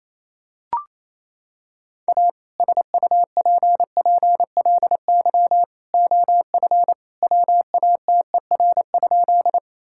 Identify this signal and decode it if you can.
Morse code, heard as a keyed tone, about 28 wpm, 700 Hz. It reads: A SUPPLY OF WATER?
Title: Sentences from Top 1000 Words - Review Part 5 - 28wpm